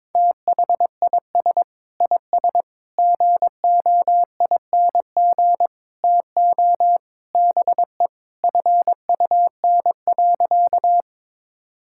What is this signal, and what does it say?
Morse code, heard as a keyed tone, about 22 wpm, 700 Hz. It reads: THIS IS GOING TO BE FUN.
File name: practice / 22wpm / Sentences with 4-Word Phrases - 22wpm